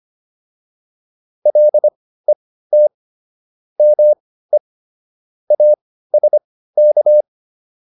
Morse code, heard as a keyed tone, about 25 wpm, spaced as for 15 wpm, 600 Hz. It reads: LET ME ASK